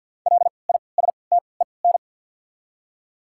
Morse code, heard as a keyed tone, about 48 wpm, spaced as for 28 wpm, 700 Hz. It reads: LISTEN
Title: 3x Character Spacing - Top 500 Words - 50wpm